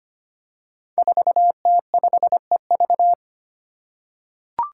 Morse code, heard as a keyed tone, about 25 wpm, 700 Hz. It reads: 4T5EV
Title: Five Challenging Letter-Number Combinations 25wpm